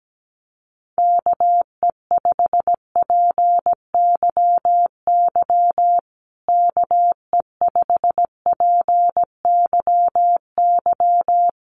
Morse code, heard as a keyed tone, about 17 wpm, 700 Hz. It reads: KE5PYY KE5PYY
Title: USA Call Signs Repeated 2x 17wpm